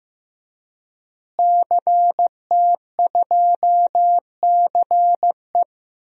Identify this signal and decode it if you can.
Morse code, heard as a keyed tone, about 15 wpm, 700 Hz. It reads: CT2CE
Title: DXCC-100 Call Signs - 5 Characters - Repeated 3x - 15wpm